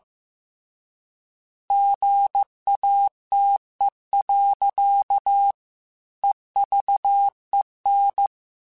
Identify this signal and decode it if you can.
Morse code, heard as a keyed tone, about 15 wpm, 800 Hz. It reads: GATE. EVEN
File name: Technical Subjects / Technical Subjects-mp3s / AND Gate @13